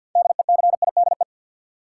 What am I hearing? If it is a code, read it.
Morse code, heard as a keyed tone, about 53 wpm, 700 Hz. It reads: DECIDE
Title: Sentences from Top 500 Words - New Word Drill - 50wpm